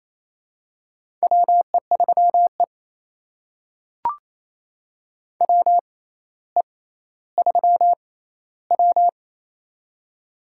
Morse code, heard as a keyed tone, about 28 wpm, spaced as for 14 wpm, 700 Hz. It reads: WE3E WE3W
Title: USA Call Signs - 4 Characters - 6x Character Spacing - 28wpm